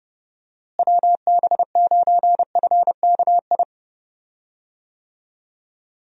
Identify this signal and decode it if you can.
Morse code, heard as a keyed tone, about 30 wpm, 700 Hz. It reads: WB9FKI
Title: USA Call Signs - 6 Characters - Repeated 3x - 30wpm